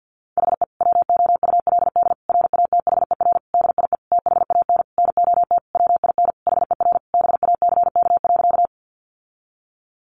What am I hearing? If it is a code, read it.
Morse code, heard as a keyed tone, about 50 wpm, 700 Hz. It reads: HE WOULD RATHER DIE THAN NOT WIN HER BACK.
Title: Sentences from Top 1500 Words 50wpm